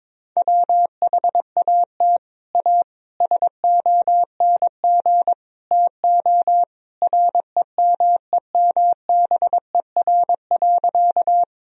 Morse code, heard as a keyed tone, about 21 wpm, 700 Hz. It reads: WHAT A SONG TO REMEMBER.
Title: Sentences from Top 1000 Words - Review Part 2 - 22wpm